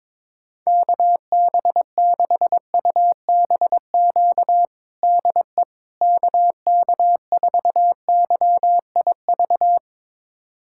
Morse code, heard as a keyed tone, about 21 wpm, 700 Hz. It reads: KB6UBQ DE KK4YIV
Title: USA Callsign DE Callsign - 6 Characters - 22wpm